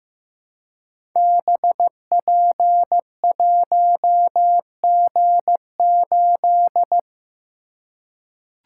Morse code, heard as a keyed tone, about 15 wpm, 700 Hz. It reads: BP1G8
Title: Code Groups - 5 Letter:Numbers - 15wpm